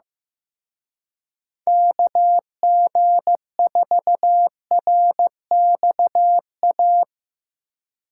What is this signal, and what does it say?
Morse code, heard as a keyed tone, about 16 wpm, 700 Hz. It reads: KG4RXA
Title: Speed Racing - USA Call Signs 15wpm